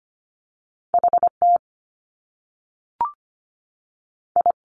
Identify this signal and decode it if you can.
Morse code, heard as a keyed tone, about 25 wpm, 700 Hz. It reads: H T I
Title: Top Two Letter Combinations in Words 25wpm